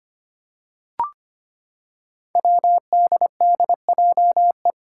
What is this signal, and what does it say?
Morse code, heard as a keyed tone, about 25 wpm, 700 Hz. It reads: WDDJE
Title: Five Challenging Letter-Number Combinations 25wpm